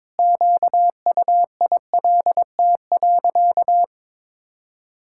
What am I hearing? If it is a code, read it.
Morse code, heard as a keyed tone, about 22 wpm, 700 Hz. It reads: QUILT.
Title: Pangram Challenge v2 - 22wpm